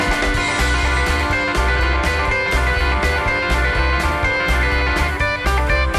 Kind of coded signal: Emergency Alert System (SAME) tones